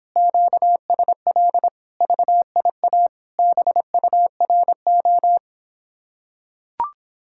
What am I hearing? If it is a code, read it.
Morse code, heard as a keyed tone, about 26 wpm, 700 Hz. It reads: QSL VIA BURO